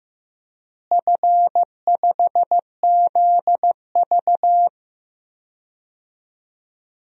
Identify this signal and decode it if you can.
Morse code, heard as a keyed tone, about 15 wpm, 700 Hz. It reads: F5ZV
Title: DXCC-100 Call Signs - 4 Characters - Repeated 2x - 15wpm